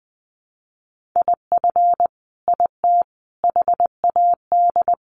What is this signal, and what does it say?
Morse code, heard as a keyed tone, about 20 wpm, 700 Hz. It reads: IF IT HAD
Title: Sets of 5 Words Encore 20wpm